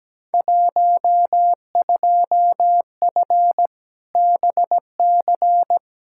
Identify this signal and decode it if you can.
Morse code, heard as a keyed tone, about 17 wpm, 700 Hz. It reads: 12F BC
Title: Field Day Exchanges Encore - 17wpm